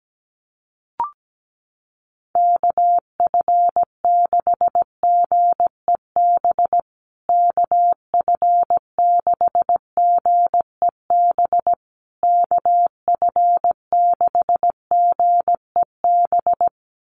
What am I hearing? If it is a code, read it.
Morse code, heard as a keyed tone, about 17 wpm, 700 Hz. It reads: KF6GEB KF6GEB KF6GEB
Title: USA Call Signs Repeated 3x 17wpm